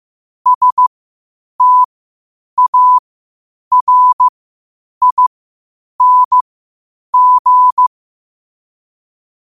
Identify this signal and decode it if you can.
Morse code, heard as a keyed tone, about 15 wpm, spaced as for 8 wpm, 1000 Hz. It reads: STARING
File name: qotdfiles / 8wpm